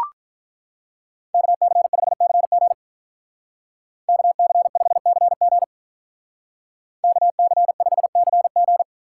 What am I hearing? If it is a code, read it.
Morse code, heard as a keyed tone, about 43 wpm, 700 Hz. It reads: KC5CG KC5CG KC5CG
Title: Speed Racing - USA 2x2 Call Signs 35wpm